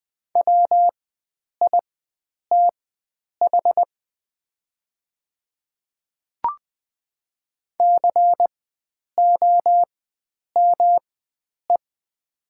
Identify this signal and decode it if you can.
Morse code, heard as a keyed tone, about 20 wpm, spaced as for 9 wpm, 700 Hz. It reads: WITH COME
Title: 4x Character Spacing - Top 100 Words - 20wpm